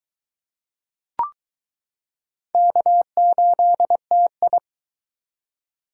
Morse code, heard as a keyed tone, about 22 wpm, 700 Hz. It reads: K8TI